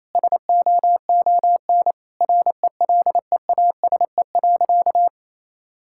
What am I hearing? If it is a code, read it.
Morse code, heard as a keyed tone, about 28 wpm, 700 Hz. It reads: SOON RELEASE.
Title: Sentences from Top 1600 Words 28wpm